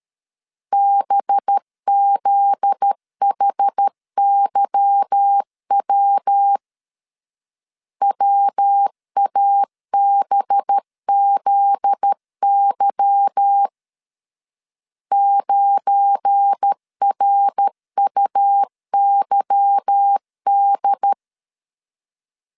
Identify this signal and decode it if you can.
Morse code, heard as a keyed tone, about 13 wpm, 800 Hz. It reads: BZHYW WABZY 9RUYD
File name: LEZIONI / 8 YJ45 / 8 Y 12wpm 2228